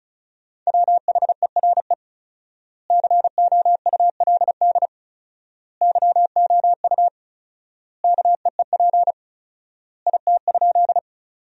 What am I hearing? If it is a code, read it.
Morse code, heard as a keyed tone, about 34 wpm, 700 Hz. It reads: WHERE COULD YOU KEEP IT?
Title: 4x Word-Spacing - Sentences from Top 300 Words - 35wpm